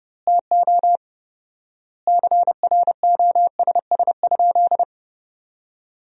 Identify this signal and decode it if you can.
Morse code, heard as a keyed tone, about 30 wpm, 700 Hz. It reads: TO CROSS?
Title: 4x Word-Spacing - Sentences from Top 300 Words - 30wpm